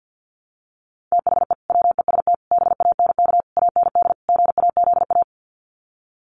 Morse code, heard as a keyed tone, about 50 wpm, 700 Hz. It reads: THE WEST BANK AND GAZA
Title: Sets of 5 Words 50wpm